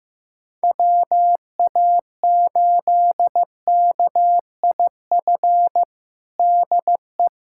Morse code, heard as a keyed tone, about 14 wpm, 700 Hz. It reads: WA8KIF DE
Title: USA Callsign DE Callsign - 6 Characters - 15wpm